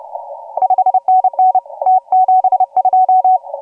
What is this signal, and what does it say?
Morse code, heard as a keyed tone, about 36 wpm, 750 Hz. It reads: 5NN T72
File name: AUDIO2018 / HG1G CW